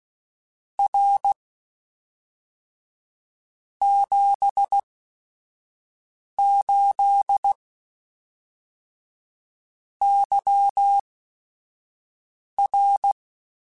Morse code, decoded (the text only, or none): R 78 YR